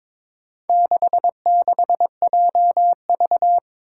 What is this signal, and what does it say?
Morse code, heard as a keyed tone, about 22 wpm, 700 Hz. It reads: 66JV
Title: Four Challenging Letter-Number Combinations 22wpm